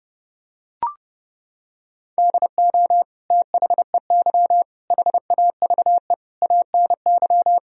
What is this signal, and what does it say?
Morse code, heard as a keyed tone, about 30 wpm, 700 Hz. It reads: DO THEY HAVE ANY
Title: Sentences from Top 1000 Words - Review Part 3 - 30wpm